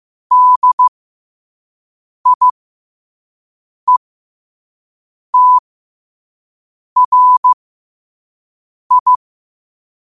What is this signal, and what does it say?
Morse code, heard as a keyed tone, about 15 wpm, spaced as for 5 wpm, 1000 Hz. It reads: DIETRI
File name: qotdfiles / 5wpm